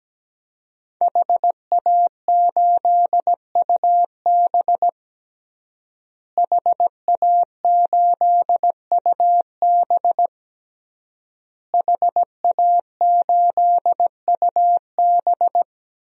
Morse code, heard as a keyed tone, about 17 wpm, 700 Hz. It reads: HA8UB HA8UB HA8UB
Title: DXCC-100 Call Signs - 5 Characters - Repeated 3x - 17wpm